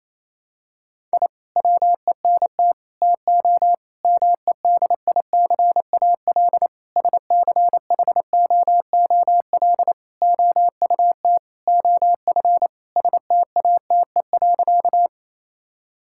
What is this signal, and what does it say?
Morse code, heard as a keyed tone, about 28 wpm, 700 Hz. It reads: I WENT TO MEDICAL SCHOOL OUT OF STATE.
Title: Sentences from Top 1000 Words - Review Part 5 - 28wpm